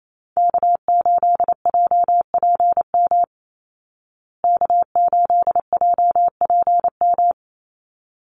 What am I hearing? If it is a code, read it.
Morse code, heard as a keyed tone, about 28 wpm, 700 Hz. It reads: K8JPM K8JPM
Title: Call Signs Repeated 3x - 28wpm